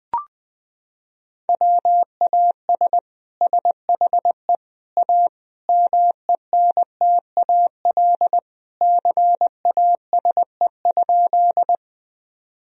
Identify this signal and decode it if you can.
Morse code, heard as a keyed tone, about 21 wpm, 700 Hz. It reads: WAS SHE A MENTAL CASE?
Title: Sentences from Top 1000 Words - Review Part 4 - 20wpm